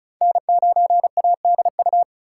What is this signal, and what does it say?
Morse code, heard as a keyed tone, about 36 wpm, 700 Hz. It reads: N9ADU
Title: USA Call Signs - 5 Characters - 2x Character Spacing - 35wpm